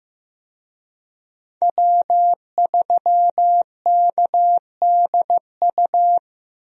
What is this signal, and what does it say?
Morse code, heard as a keyed tone, about 15 wpm, 700 Hz. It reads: W3KDU
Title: USA Call Signs - 5 Characters - Repeated 3x - 15wpm